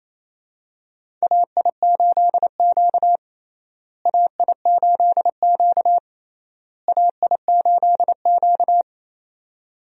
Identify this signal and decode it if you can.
Morse code, heard as a keyed tone, about 28 wpm, 700 Hz. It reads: AI8Q AI8Q AI8Q
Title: USA Call Signs - 4 Characters - Repeated 3x - 28wpm